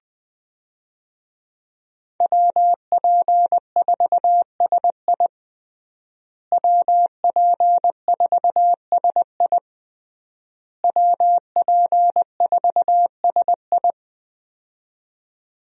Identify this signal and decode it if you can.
Morse code, heard as a keyed tone, about 21 wpm, 700 Hz. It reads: WP4SI WP4SI WP4SI
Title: USA Call Signs - 5 Characters - Repeated 3x - 20wpm